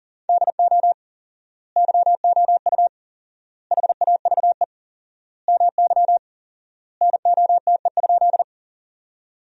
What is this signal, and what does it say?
Morse code, heard as a keyed tone, about 40 wpm, 700 Hz. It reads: DO YOU HAVE MY NOTE?
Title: 4x Word-Spacing - Sentences from Top 500 Words - 40wpm